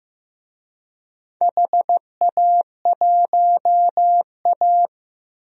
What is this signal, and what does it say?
Morse code, heard as a keyed tone, about 15 wpm, 700 Hz. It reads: HA1A
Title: DXCC-100 Call Signs - 4 Characters - 15wpm